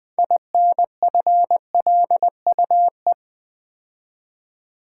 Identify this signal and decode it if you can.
Morse code, heard as a keyed tone, about 20 wpm, 700 Hz. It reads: INFLUE
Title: Build A Word - 3 Letters At A Time - 9-Letter Words - 20wpm